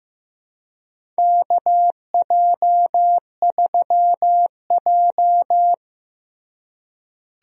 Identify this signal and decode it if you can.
Morse code, heard as a keyed tone, about 15 wpm, 700 Hz. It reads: KJ3J
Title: USA Call Signs - 4 Characters - Repeated 3x - 15wpm